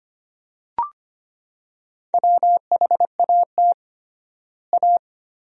Morse code, heard as a keyed tone, about 25 wpm, 700 Hz. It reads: WHAT A
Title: 3x Word-Spacing - Sentences from Top 300 Words - 25wpm